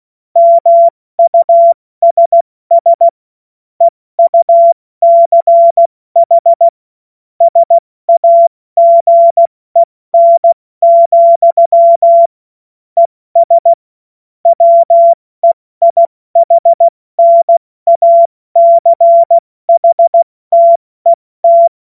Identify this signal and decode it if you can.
Morse code, heard as a keyed tone, about 16 wpm, 680 Hz. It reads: MUSS EUCH SAGEN, ES WEIHNACHTET
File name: Morsekurs / Witze / Witz 14. März 2023 70 F80